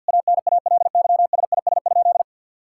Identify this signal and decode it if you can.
Morse code, heard as a keyed tone, about 50 wpm, 700 Hz. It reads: ANALYSIS?
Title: Sentences from Top 1000 Words - Review Part 5 - 50wpm